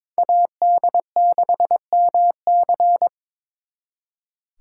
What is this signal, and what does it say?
Morse code, heard as a keyed tone, about 26 wpm, 700 Hz. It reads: AD6MC
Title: Speed Racing - USA Call Signs 22wpm